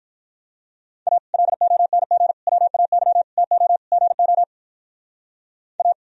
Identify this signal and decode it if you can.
Morse code, heard as a keyed tone, about 44 wpm, 700 Hz. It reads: A LONG WAY TO GO A